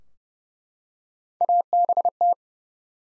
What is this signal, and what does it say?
Morse code, heard as a keyed tone, about 32 wpm, 700 Hz. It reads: ABT
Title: Top 90 Abbr 30WPM